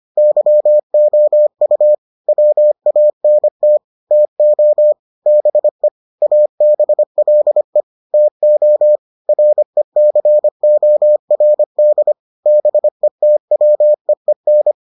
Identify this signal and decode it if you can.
Morse code, heard as a keyed tone, about 25 wpm, 600 Hz. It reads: YOU WANT TO BE ABLE TO RECORD BETWEEN